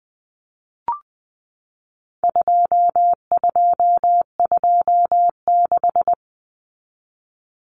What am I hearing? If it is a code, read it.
Morse code, heard as a keyed tone, about 20 wpm, 700 Hz. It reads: 2226
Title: Common Cut Numbers - 4-Digit 20wpm